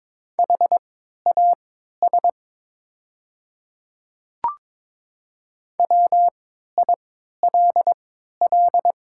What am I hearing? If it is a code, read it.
Morse code, heard as a keyed tone, about 22 wpm, spaced as for 13 wpm, 700 Hz. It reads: HAS WILL
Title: 3x Character Spacing - Top 100 Words - 22wpm